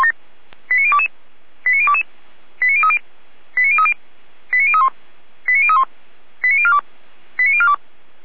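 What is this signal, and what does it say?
Five-tone selective call (ZVEI), heard as quick runs of five tones, tones of about 80 ms. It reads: 89022 89022 89030 89030 89031 89031 89042 89042
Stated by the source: ZVEI five-tone selective call 89014 received by SDR.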